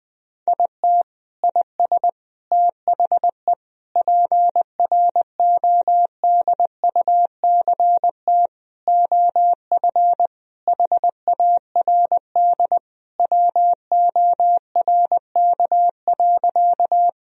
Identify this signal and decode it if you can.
Morse code, heard as a keyed tone, about 20 wpm, 700 Hz. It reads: IT IS THE PRODUCT OF HARD WORK.
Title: Sentences from Top 1000 Words - Review Part 2 - 20wpm